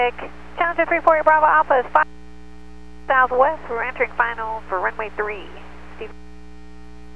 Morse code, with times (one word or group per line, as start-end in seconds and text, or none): none